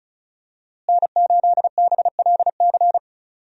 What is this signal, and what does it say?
Morse code, heard as a keyed tone, about 36 wpm, 700 Hz. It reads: N8BLC
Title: USA Call Signs - 5 Characters - Repeated 2x - 35wpm